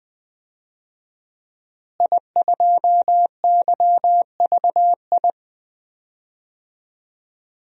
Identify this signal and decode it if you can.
Morse code, heard as a keyed tone, about 20 wpm, 700 Hz. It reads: I2YVI
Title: DXCC-100 Call Signs - 5 Characters - Repeated 3x - 20wpm